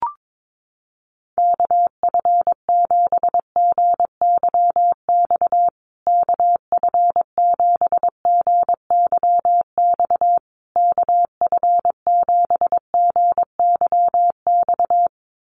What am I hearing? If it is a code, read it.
Morse code, heard as a keyed tone, about 22 wpm, 700 Hz. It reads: KF7GYX KF7GYX KF7GYX
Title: USA Call Signs Repeated 3x 22wpm